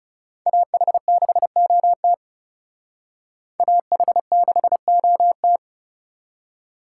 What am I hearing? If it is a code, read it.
Morse code, heard as a keyed tone, about 39 wpm, 700 Hz. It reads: AH6OT AH6OT